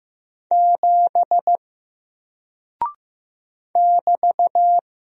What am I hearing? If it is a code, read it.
Morse code, heard as a keyed tone, about 15 wpm, 700 Hz. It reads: 7 <BT>
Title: Single Character (Letter:Number:Punctuation:Prosign) 15wpm